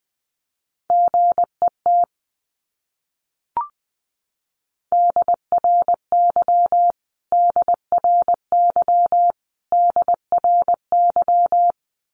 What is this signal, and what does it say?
Morse code, heard as a keyed tone, about 20 wpm, 700 Hz. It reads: GET DRY DRY DRY
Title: Top 500 Words Repeated 20wpm